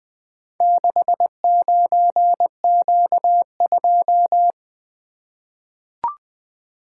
Morse code, heard as a keyed tone, about 20 wpm, 700 Hz. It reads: 69Q2
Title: Four Letter-Number Combinations - 20wpm